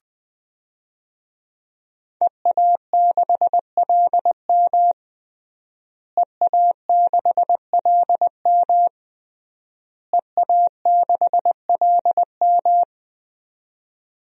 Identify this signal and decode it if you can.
Morse code, heard as a keyed tone, about 20 wpm, 700 Hz. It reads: EA6LM EA6LM EA6LM
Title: DXCC-100 Call Signs - 5 Characters - Repeated 3x - 20wpm